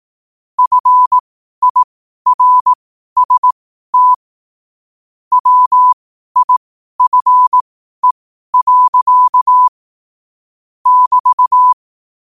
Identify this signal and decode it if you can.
Morse code, heard as a keyed tone, about 18 wpm, spaced as for 12 wpm, 1000 Hz. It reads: FIRST WIFE. <BT>
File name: qotdfiles / 12wpm